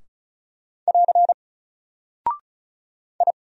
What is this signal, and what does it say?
Morse code, heard as a keyed tone, about 36 wpm, 700 Hz. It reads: <AR> I
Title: Single Character (Letter:Number:Punctuation:Prosign) Alt 35wpm